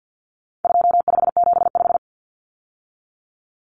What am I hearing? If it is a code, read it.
Morse code, heard as a keyed tone, about 48 wpm, 700 Hz. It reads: 2575
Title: Numbers - 4-Digit 50wpm